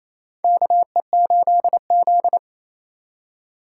Morse code, heard as a keyed tone, about 28 wpm, 700 Hz. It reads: KE8Z
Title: USA Call Signs - 4 Characters - 3x Character Spacing - 28wpm